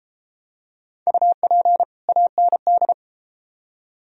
Morse code, heard as a keyed tone, about 31 wpm, 700 Hz. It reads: UP AND